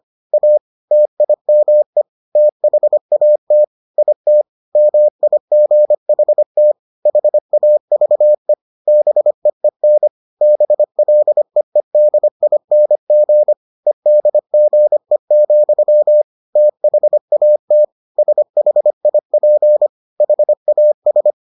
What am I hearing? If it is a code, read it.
Morse code, heard as a keyed tone, about 25 wpm, 600 Hz. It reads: A TIME THAT IT MIGHT HAVE BEEN BLEEDING EDGE, THAT SHIP HAS